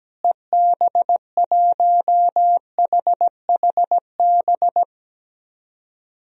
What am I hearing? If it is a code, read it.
Morse code, heard as a keyed tone, about 17 wpm, 700 Hz. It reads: EB1HHB
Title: DXCC-100 Call Signs - 6 Characters - Repeated 2x - 17wpm